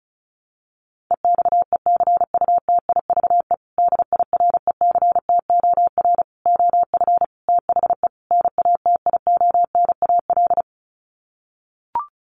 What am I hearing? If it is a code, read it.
Morse code, heard as a keyed tone, about 34 wpm, 700 Hz. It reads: EXECUTIVE DIRECTOR OF THE NATIONAL
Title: Sets of 5 Words Encore2 35wpm